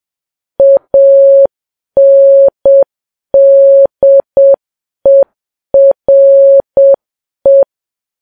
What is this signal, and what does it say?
Morse code, heard as a keyed tone, about 7 wpm, 550 Hz. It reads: ANDERE
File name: Files Web tls / Morse on9fv / morse7kps